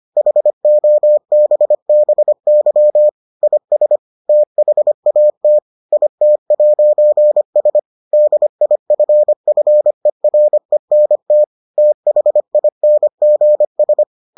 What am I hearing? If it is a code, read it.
Morse code, heard as a keyed tone, about 25 wpm, 600 Hz. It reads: HOBBY IS THAT IT'S DIFFERENT THINGS